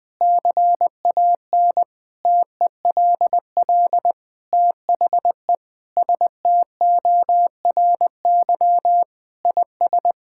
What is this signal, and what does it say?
Morse code, heard as a keyed tone, about 20 wpm, 700 Hz. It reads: CAN TELL THE STORY IS
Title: Sentences with 7-Word Phrases - Spoken Beginning - 20wpm